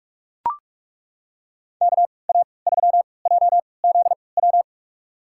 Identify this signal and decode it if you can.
Morse code, heard as a keyed tone, about 42 wpm, spaced as for 25 wpm, 700 Hz. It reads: KA3JZW
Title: USA Call Signs - 6 Characters - 3x Character Spacing - 45wpm